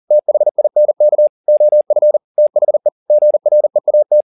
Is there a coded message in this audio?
Yes: Morse code from 0.1 to 4.2 s, heard as a keyed tone, about 40 wpm, 600 Hz. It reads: THINK OF THE GREAT